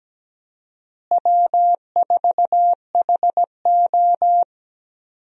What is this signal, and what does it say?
Morse code, heard as a keyed tone, about 17 wpm, 700 Hz. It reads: W4HO
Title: USA Call Signs - 4 Characters - Repeated 2x - 17wpm